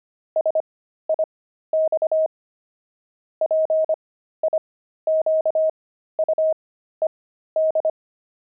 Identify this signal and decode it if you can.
Morse code, heard as a keyed tone, about 25 wpm, spaced as for 13 wpm, 630 Hz. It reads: SIX PIQUED
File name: pangram-fs-13-cs-25-frogs